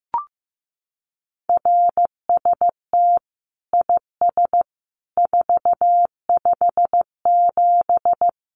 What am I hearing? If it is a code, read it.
Morse code, heard as a keyed tone, about 15 wpm, 700 Hz. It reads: RST IS 457
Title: Common QSO Phrases 15wpm